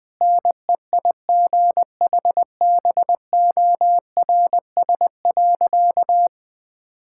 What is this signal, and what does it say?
Morse code, heard as a keyed tone, about 20 wpm, 700 Hz. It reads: NEIGHBORS.
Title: Sentences with 3-Word Phrases - 20wpm